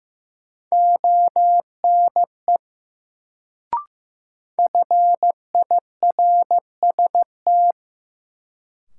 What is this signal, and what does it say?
Morse code, heard as a keyed tone, about 15 wpm, 700 Hz. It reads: ONE FIRST
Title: Sets of 2 Words - New Word Drill - 15wpm